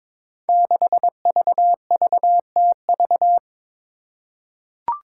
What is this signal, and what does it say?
Morse code, heard as a keyed tone, about 22 wpm, 700 Hz. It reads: 6VVTV
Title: Five Challenging Letter-Number Combinations 22wpm